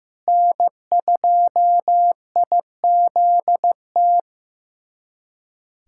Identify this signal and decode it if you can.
Morse code, heard as a keyed tone, about 15 wpm, 700 Hz. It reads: N2IZT
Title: USA Callsign DE Callsign - 5 Characters - 15wpm